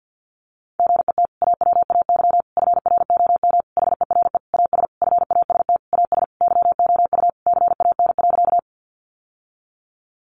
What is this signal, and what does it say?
Morse code, heard as a keyed tone, about 50 wpm, 700 Hz. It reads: GET AWAY FROM HERE AS FAST AS YOU CAN.
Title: Sentences from Top 600 Words 50wpm